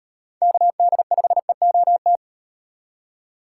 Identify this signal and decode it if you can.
Morse code, heard as a keyed tone, about 38 wpm, 700 Hz. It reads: KD5EOT